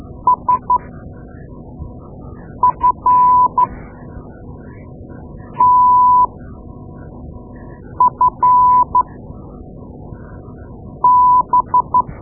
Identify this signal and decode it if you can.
Morse code, heard as a keyed tone, about 13 wpm, spaced as for 4 wpm, 1000 Hz. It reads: SFTFB
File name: Morse Lessons / Morse Lessons / Tape-2A